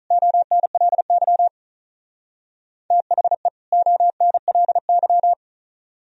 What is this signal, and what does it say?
Morse code, heard as a keyed tone, about 35 wpm, 700 Hz. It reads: ONLY THE ONLY
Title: Speed-Racing - Sets of 2 Words 35wpm